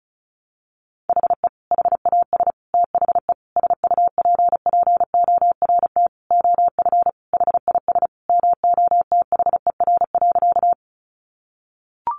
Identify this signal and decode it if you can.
Morse code, heard as a keyed tone, about 34 wpm, 700 Hz. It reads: HE HAS THE SUPPORT OF HIS MOTHER.
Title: Sentences from Top 800 Words 35wpm